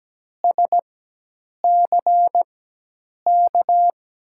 Morse code, heard as a keyed tone, about 17 wpm, spaced as for 8 wpm, 700 Hz. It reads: SCK
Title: USA Call Signs - 6 Characters - 4x Character Spacing - 17wpm